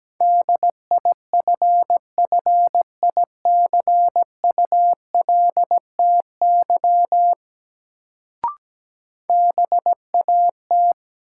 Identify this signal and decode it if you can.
Morse code, heard as a keyed tone, about 17 wpm, 700 Hz. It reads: DIFFICULTY BAT
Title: Sentences from Top 1500 Words - New Word Drill - 17wpm